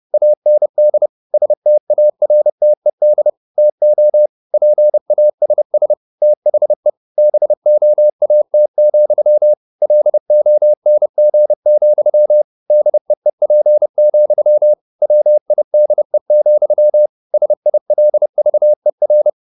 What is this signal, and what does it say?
Morse code, heard as a keyed tone, about 30 wpm, 600 Hz. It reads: AND STARTED TO PASS THE BOAT, LONG, DEEP, WIDE, SILVER